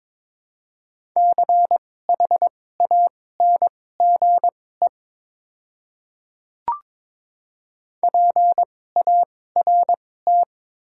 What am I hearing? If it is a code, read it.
Morse code, heard as a keyed tone, about 22 wpm, spaced as for 16 wpm, 700 Hz. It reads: CHANGE PART
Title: 2x Character Spacing - Top 300 Words - 22wpm